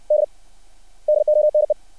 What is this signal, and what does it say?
Morse code, heard as a keyed tone, about 60 wpm, 600 Hz. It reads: D DONE
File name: archived / CRSnet